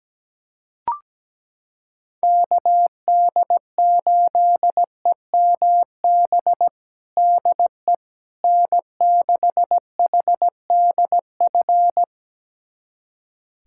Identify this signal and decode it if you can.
Morse code, heard as a keyed tone, about 17 wpm, 700 Hz. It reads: KD8EMB DE N6HDF
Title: USA Callsign DE Callsign 17wpm